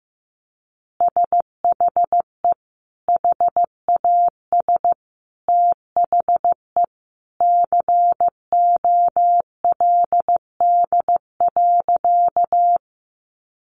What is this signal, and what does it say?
Morse code, heard as a keyed tone, about 15 wpm, 700 Hz. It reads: SHE HAS THE COLD.
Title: Easy Phrases 15wpm